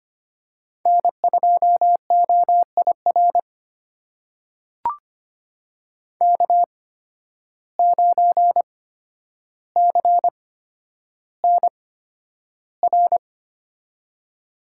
Morse code, heard as a keyed tone, about 25 wpm, spaced as for 11 wpm, 700 Hz. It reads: N2OIR K9CNR
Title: USA Call Signs - 8x Character Spacing - 25wpm